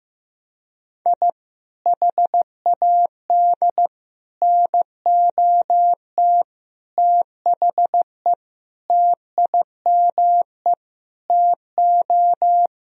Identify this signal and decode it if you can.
Morse code, heard as a keyed tone, about 15 wpm, 700 Hz. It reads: I HAD NOT THE TIME TO
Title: Sentences from Top 1000 Words - Review Part 1 - 15wpm